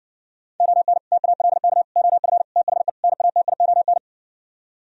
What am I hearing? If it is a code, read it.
Morse code, heard as a keyed tone, about 70 wpm, 700 Hz. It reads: ON TALK OF THE NATION